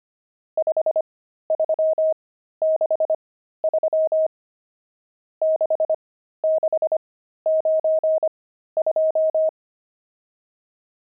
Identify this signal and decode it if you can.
Morse code, heard as a keyed tone, about 25 wpm, spaced as for 13 wpm, 630 Hz. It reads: 5363 6692
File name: pangram-fs-13-cs-25-nums7